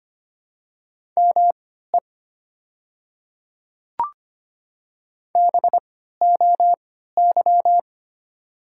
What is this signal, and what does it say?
Morse code, heard as a keyed tone, about 25 wpm, spaced as for 14 wpm, 700 Hz. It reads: ME BOY
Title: 3x Character Spacing - Top 200 Words - 25wpm